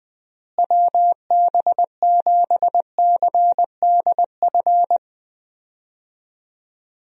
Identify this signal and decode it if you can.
Morse code, heard as a keyed tone, about 21 wpm, 700 Hz. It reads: WB7CDF
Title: USA Call Signs - 6 Characters - Repeated 3x - 20wpm